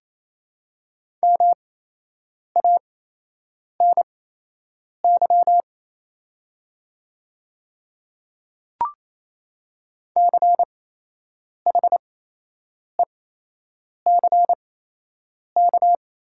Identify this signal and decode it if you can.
Morse code, heard as a keyed tone, about 28 wpm, spaced as for 8 wpm, 700 Hz. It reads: MANY CHECK
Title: 8x Character Spacing - Top 500 Words - 28wpm